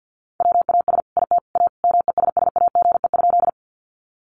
Morse code, heard as a keyed tone, about 50 wpm, 700 Hz. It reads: WAS IT A MESSAGE?
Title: Sentences from Top 1100 Words 50wpm